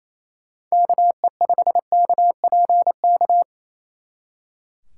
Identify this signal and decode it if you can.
Morse code, heard as a keyed tone, about 36 wpm, 700 Hz. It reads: KE5KPK